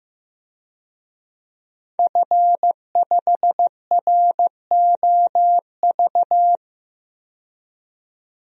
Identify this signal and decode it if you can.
Morse code, heard as a keyed tone, about 15 wpm, 700 Hz. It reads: F5ROV
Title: DXCC-100 Call Signs - 5 Characters - 15wpm